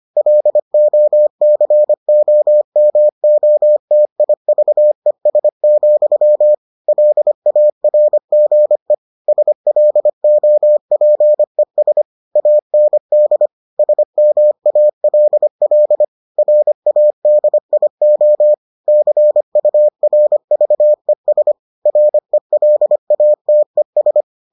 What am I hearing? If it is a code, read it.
Morse code, heard as a keyed tone, about 25 wpm, 600 Hz. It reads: LOCOMOTIVES, LARGE SLOPES AND SMALL RADIO CURVES RELATES